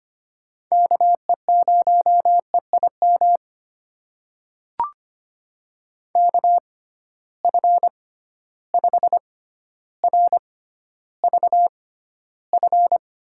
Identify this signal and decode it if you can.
Morse code, heard as a keyed tone, about 25 wpm, spaced as for 13 wpm, 700 Hz. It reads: KE0EIM KF5RVF
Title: USA Call Signs - 6 Characters - 6x Character Spacing - 25wpm